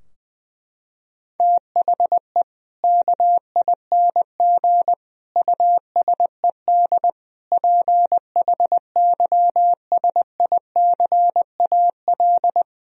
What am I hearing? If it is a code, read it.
Morse code, heard as a keyed tone, about 20 wpm, 700 Hz. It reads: THE KING USED PHYSICAL
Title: Sentences from Top 1000 Words - Review Part 4 - 20wpm